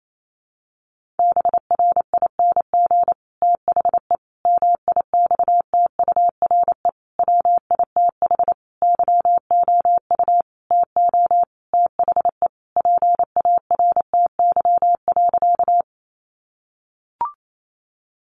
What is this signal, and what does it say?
Morse code, heard as a keyed tone, about 28 wpm, 700 Hz. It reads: BRING THE MIXTURE WITH YOU TO THE PARTY.
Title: Sentences from Top 1500 Words 28wpm